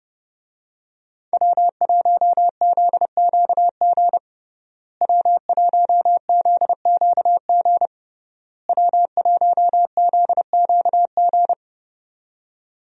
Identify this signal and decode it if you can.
Morse code, heard as a keyed tone, about 29 wpm, 700 Hz. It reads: W1ZQG W1ZQG W1ZQG
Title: USA Call Signs - 5 Characters - Repeated 3x - 30wpm